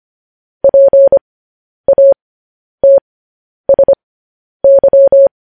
Morse code, heard as a keyed tone, about 25 wpm, spaced as for 10 wpm, 550 Hz. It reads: PATSY
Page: news headlines in morse code at 10 wpm